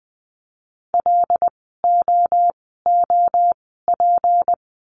Morse code, heard as a keyed tone, about 20 wpm, spaced as for 15 wpm, 700 Hz. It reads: LOOP
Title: 153 Lesson 25f - Letter p - Words 20wpm